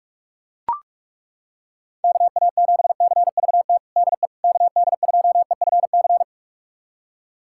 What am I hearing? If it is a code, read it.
Morse code, heard as a keyed tone, about 41 wpm, 700 Hz. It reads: KA7CVT DE KD2EFC
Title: USA Callsign DE Callsign 45wpm